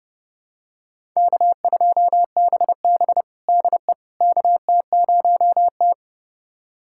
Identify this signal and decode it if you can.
Morse code, heard as a keyed tone, about 30 wpm, 700 Hz. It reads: K2BB DE KT0T
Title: USA Callsign DE Callsign - 4 Characters - 30wpm